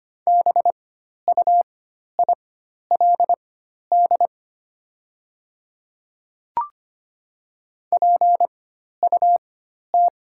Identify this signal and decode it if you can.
Morse code, heard as a keyed tone, about 25 wpm, spaced as for 12 wpm, 700 Hz. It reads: BUILD PUT
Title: 4x Character Spacing - Top 200 Words - 25wpm